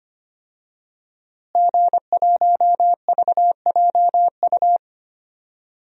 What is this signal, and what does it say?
Morse code, heard as a keyed tone, about 25 wpm, 700 Hz. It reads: G1VJU
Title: DXCC-100 Call Signs - 5 Characters - Repeated 3x - 25wpm